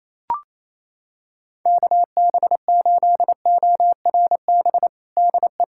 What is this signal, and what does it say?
Morse code, heard as a keyed tone, about 28 wpm, 700 Hz. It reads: KB8ORB DE
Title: USA Callsign DE Callsign - 6 Characters - 28wpm